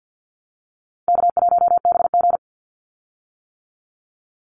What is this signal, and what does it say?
Morse code, heard as a keyed tone, about 50 wpm, 700 Hz. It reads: K1BG
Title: Easy Call Signs - 50wpm